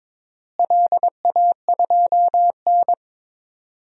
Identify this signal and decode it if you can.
Morse code, heard as a keyed tone, about 22 wpm, 700 Hz. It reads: LA2N
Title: DXCC-100 Call Signs - 4 Characters - Repeated 2x - 22wpm